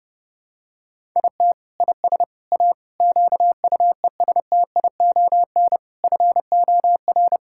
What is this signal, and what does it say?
Morse code, heard as a keyed tone, about 32 wpm, 700 Hz. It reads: IT IS A QUESTION FOR
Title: Sentences from Top 1000 Words - Review Part 6 - 30wpm